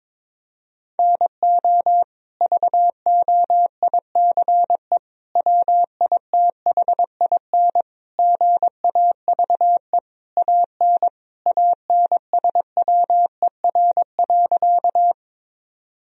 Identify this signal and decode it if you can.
Morse code, heard as a keyed tone, about 22 wpm, 700 Hz. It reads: NO VOICE WITHIN GAVE AN ANSWER.
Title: Sentences from Top 1000 Words - Review Part 4 - 22wpm